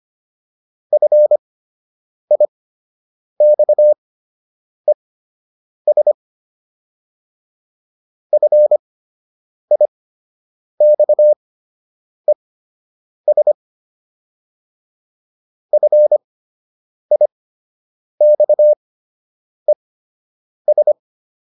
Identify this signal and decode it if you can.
Morse code, heard as a keyed tone, about 25 wpm, spaced as for 8 wpm, 600 Hz. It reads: FIXES FIXES FIXES